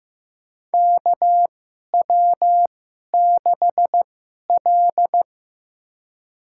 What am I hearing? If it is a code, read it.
Morse code, heard as a keyed tone, about 15 wpm, spaced as for 11 wpm, 700 Hz. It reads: KW6L
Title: USA Call Signs - 2x Character Spacing - 15wpm